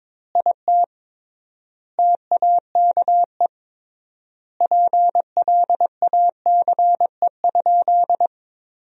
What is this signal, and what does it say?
Morse code, heard as a keyed tone, about 22 wpm, 700 Hz. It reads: IT TAKE PLACE?
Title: 3x Word-Spacing - Sentences from Top 200 Words - 22wpm